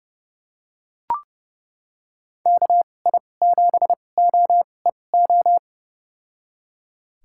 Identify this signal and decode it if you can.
Morse code, heard as a keyed tone, about 30 wpm, spaced as for 22 wpm, 700 Hz. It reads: KI7OEO
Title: USA Call Signs - 6 Characters - 2x Character Spacing - 30wpm